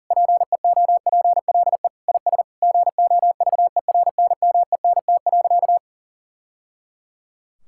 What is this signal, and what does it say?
Morse code, heard as a keyed tone, about 40 wpm, 700 Hz. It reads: PEOPLE IS GOVERNMENT.
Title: Sentences from Top 1000 Words - Review Part 6 - 40wpm